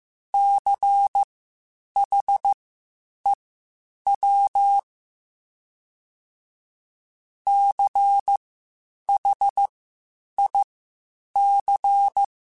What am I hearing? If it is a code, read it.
Morse code, heard as a keyed tone, about 15 wpm, spaced as for 8 wpm, 800 Hz. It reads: CHEW CHIC
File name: Spaced Alphas / Spaced Alphas mp3s / Cs sp @13